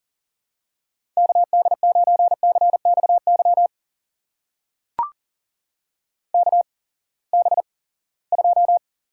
Morse code, heard as a keyed tone, about 40 wpm, spaced as for 23 wpm, 700 Hz. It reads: KD9CXY KB2
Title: USA Call Signs - 6 Characters - 8x Character Spacing - 40wpm